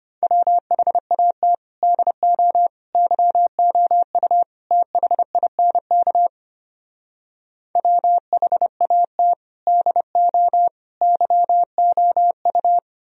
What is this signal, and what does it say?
Morse code, heard as a keyed tone, about 33 wpm, 700 Hz. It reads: WHAT DO YOU THINK WHAT DO YOU